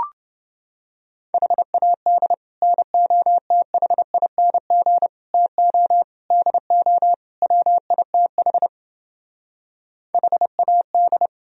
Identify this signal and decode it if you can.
Morse code, heard as a keyed tone, about 30 wpm, 700 Hz. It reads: HAD NOTHING TO DO WITH HAD